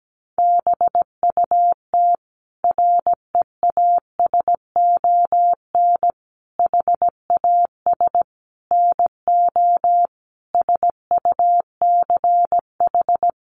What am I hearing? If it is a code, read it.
Morse code, heard as a keyed tone, about 17 wpm, 700 Hz. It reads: BUT REASON HAS NO SUCH
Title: Sentences from Top 1100 Words 17wpm